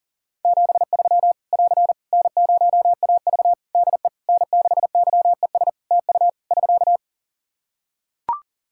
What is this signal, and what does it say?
Morse code, heard as a keyed tone, about 37 wpm, 700 Hz. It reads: 73 <AR> N0AV DE N6YES TU <SK>
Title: Practice QSO - Exchange 4 - Ending - 40wpm